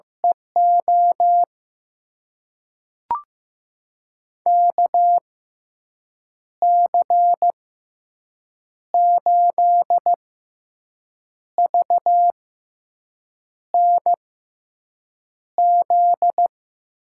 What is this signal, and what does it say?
Morse code, heard as a keyed tone, about 15 wpm, spaced as for 6 wpm, 700 Hz. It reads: EO KC8VNZ
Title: USA Call Signs - 6 Characters - 6x Character Spacing - 15wpm